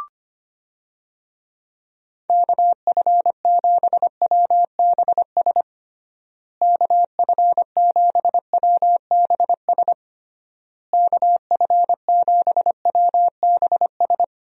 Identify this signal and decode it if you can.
Morse code, heard as a keyed tone, about 25 wpm, 700 Hz. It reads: KF7WBS KF7WBS KF7WBS
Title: USA Call Signs - 6 Characters - Repeated 3x - 25wpm